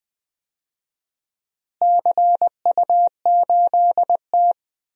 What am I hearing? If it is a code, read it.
Morse code, heard as a keyed tone, about 20 wpm, 700 Hz. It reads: CU8T
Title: DXCC-100 Call Signs - 4 Characters - Repeated 2x - 20wpm